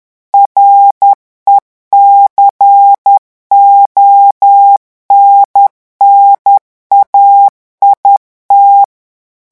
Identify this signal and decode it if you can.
Morse code, heard as a keyed tone, about 10 wpm, 800 Hz. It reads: RECONNAIT
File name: revision9 citations humoristiques vitesse 10 mots minute